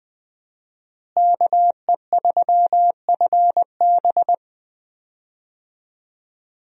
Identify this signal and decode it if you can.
Morse code, heard as a keyed tone, about 20 wpm, 700 Hz. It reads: KE3FB
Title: USA Call Signs - 5 Characters - Repeated 3x - 20wpm